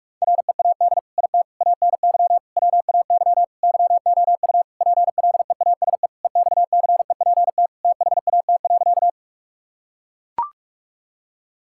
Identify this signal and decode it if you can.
Morse code, heard as a keyed tone, about 42 wpm, 700 Hz. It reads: READ IT ANY WAY YOU PLEASE EXCEPT THAT.
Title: Sentences from Top 1600 Words 45wpm